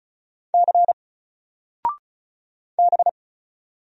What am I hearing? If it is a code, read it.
Morse code, heard as a keyed tone, about 36 wpm, 700 Hz. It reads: C B
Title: Single Letter-Number 35wpm